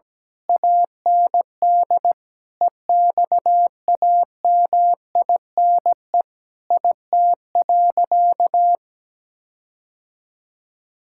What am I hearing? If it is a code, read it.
Morse code, heard as a keyed tone, about 17 wpm, 700 Hz. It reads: AND EXAMINE IT.
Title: Sentences from Top 1600 Words 17wpm